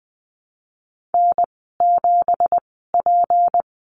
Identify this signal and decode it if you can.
Morse code, heard as a keyed tone, about 20 wpm, spaced as for 14 wpm, 700 Hz. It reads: N7P
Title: 173 Lesson 28g - Number 9 - Call Signs 20wpm